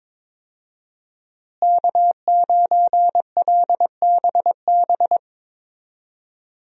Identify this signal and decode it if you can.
Morse code, heard as a keyed tone, about 22 wpm, 700 Hz. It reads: K9LBB
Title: USA Call Signs - 5 Characters - Repeated 2x - 22wpm